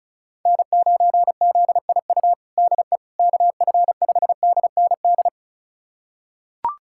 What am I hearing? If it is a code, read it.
Morse code, heard as a keyed tone, about 36 wpm, 700 Hz. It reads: N9ZIU DE KF5DND
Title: USA Callsign DE Callsign 35wpm